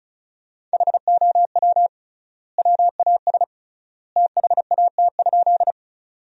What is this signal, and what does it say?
Morse code, heard as a keyed tone, about 36 wpm, 700 Hz. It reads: HOW WAS THAT?
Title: 3x Word-Spacing - Sentences from Top 100 Words - 35wpm